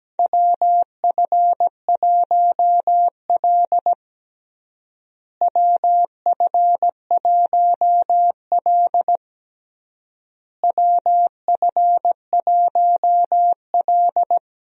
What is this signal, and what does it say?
Morse code, heard as a keyed tone, about 17 wpm, 700 Hz. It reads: WF1L WF1L WF1L
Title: USA Call Signs - 4 Characters - Repeated 3x - 17wpm